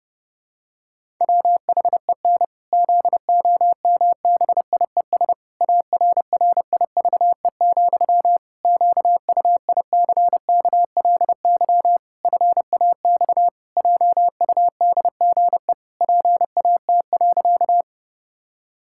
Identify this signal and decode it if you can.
Morse code, heard as a keyed tone, about 30 wpm, 700 Hz. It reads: WHEN ZOMBIES ARRIVE, QUICKLY FAX JUDGE PAT.
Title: Pangram Challenge v2 - 30wpm